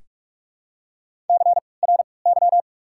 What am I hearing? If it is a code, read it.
Morse code, heard as a keyed tone, about 45 wpm, spaced as for 25 wpm, 700 Hz. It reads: CRY
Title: 3x Character Spacing - Top 500 Words - 45wpm